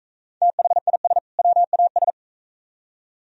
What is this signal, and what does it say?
Morse code, heard as a keyed tone, about 43 wpm, 700 Hz. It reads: THIS WAS